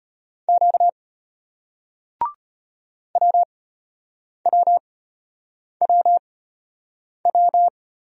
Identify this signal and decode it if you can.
Morse code, heard as a keyed tone, about 32 wpm, spaced as for 8 wpm, 700 Hz. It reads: Q WWWW